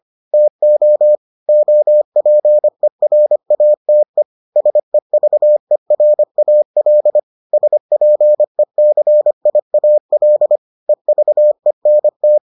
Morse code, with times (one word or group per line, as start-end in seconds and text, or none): 0.3-1.2 TO
1.5-4.2 OPERATE
4.6-7.2 SEVERAL
7.5-10.6 SPECIAL
10.9-12.4 EVENT